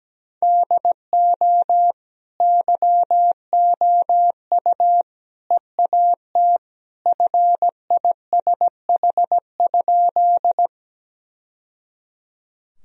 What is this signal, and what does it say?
Morse code, heard as a keyed tone, about 17 wpm, 700 Hz. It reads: DO YOU EAT FISH?
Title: Sentences from Top 1000 Words - Review Part 1 - 17wpm